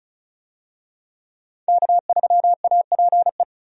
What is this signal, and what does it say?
Morse code, heard as a keyed tone, about 36 wpm, 700 Hz. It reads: K3APE
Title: USA Call Signs - 5 Characters - 35wpm